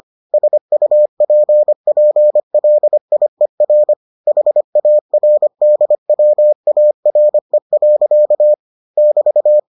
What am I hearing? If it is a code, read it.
Morse code, heard as a keyed tone, about 25 wpm, 600 Hz. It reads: SUPPLIER HARDWARE. <BT>